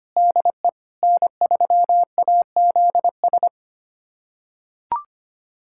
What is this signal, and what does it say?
Morse code, heard as a keyed tone, about 25 wpm, 700 Hz. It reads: DE N3AZS
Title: USA Callsign DE Callsign - 5 Characters - 25wpm